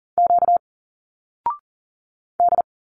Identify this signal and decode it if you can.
Morse code, heard as a keyed tone, about 39 wpm, 700 Hz. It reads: Q D
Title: Single Letters 40wpm